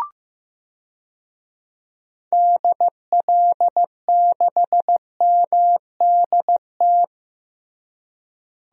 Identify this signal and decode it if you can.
Morse code, heard as a keyed tone, about 15 wpm, 700 Hz. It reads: DL6MDT
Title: DXCC-100 Call Signs - 6 Characters - Repeated 2x - 15wpm